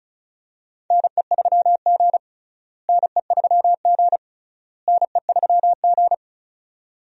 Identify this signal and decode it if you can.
Morse code, heard as a keyed tone, about 34 wpm, 700 Hz. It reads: NE3G NE3G NE3G
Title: USA Call Signs - 4 Characters - Repeated 3x - 35wpm